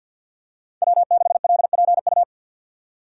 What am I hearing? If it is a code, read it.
Morse code, heard as a keyed tone, about 46 wpm, 700 Hz. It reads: W6LPU
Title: USA Call Signs - 5 Characters - 6x Character Spacing - 50wpm